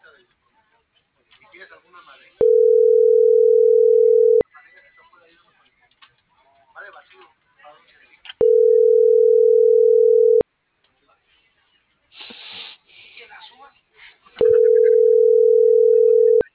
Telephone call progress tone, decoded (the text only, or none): ringback tone